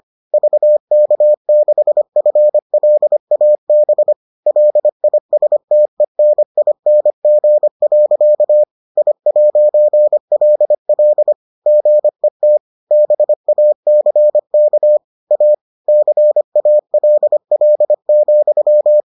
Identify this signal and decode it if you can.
Morse code, heard as a keyed tone, about 25 wpm, 600 Hz. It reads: VK6FLAB LISTENING. I'LL GET BACK A CALL,